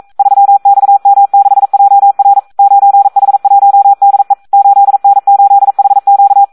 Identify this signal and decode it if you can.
Morse code, heard as a keyed tone, about 42 wpm, 800 Hz. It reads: 3XM6JR 9H1DE 8N8HQ
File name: Audio practica CW / SD 1 42